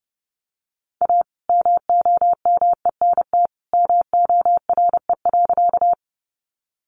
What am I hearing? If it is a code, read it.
Morse code, heard as a keyed tone, about 30 wpm, 700 Hz. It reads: A MOMENT MORE.
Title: Sentences from Top 1000 Words 30wpm